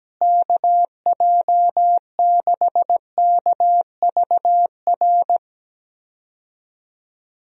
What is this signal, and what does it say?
Morse code, heard as a keyed tone, about 17 wpm, 700 Hz. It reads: KJ6KVR
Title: USA Call Signs - 6 Characters - Repeated 2x - 17wpm